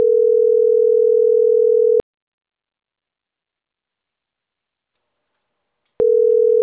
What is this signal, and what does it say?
Ringback tone (North America), 440 + 480 Hz.